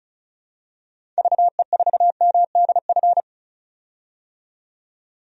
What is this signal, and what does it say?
Morse code, heard as a keyed tone, about 36 wpm, 700 Hz. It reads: VE4MDF
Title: DXCC-100 Call Signs - 6 Characters - 35wpm